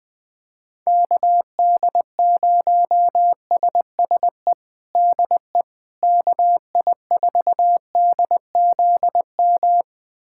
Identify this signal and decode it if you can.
Morse code, heard as a keyed tone, about 21 wpm, 700 Hz. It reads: KD0SSE DE KI4DZM
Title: USA Callsign DE Callsign 20wpm